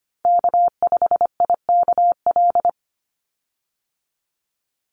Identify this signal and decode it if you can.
Morse code, heard as a keyed tone, about 25 wpm, 700 Hz. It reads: K5IKL
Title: USA Call Signs 25wpm